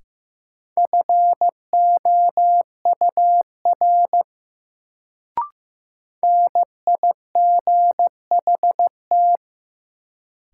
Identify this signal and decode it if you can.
Morse code, heard as a keyed tone, about 15 wpm, 700 Hz. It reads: FOUR NIGHT
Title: Sentences from Top 300 Words - New Word Drill - 15wpm